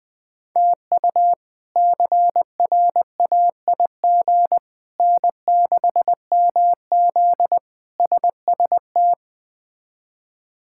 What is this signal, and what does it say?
Morse code, heard as a keyed tone, about 20 wpm, 700 Hz. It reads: TU CRAIG N6MZ SST
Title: K1USN SST QSO Practice - 20wpm